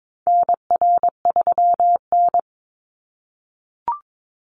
Morse code, heard as a keyed tone, about 22 wpm, 700 Hz. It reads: NR3N
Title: Call Signs - 22wpm